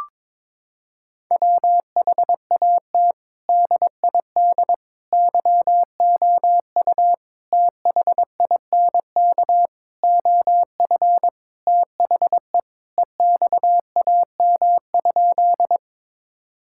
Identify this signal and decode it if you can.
Morse code, heard as a keyed tone, about 21 wpm, 700 Hz. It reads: WHAT DID YOU THINK OF THE EXAM?
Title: Sentences from Top 1000 Words - Review Part 5 - 22wpm